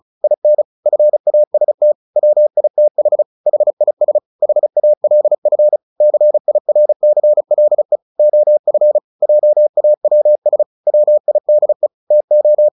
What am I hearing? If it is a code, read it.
Morse code, heard as a keyed tone, about 34 wpm, 600 Hz. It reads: IN FAST WITH HIS HALF CIRCLE OF JAWS WIDE TO
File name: the old man and the sea 2 35wpm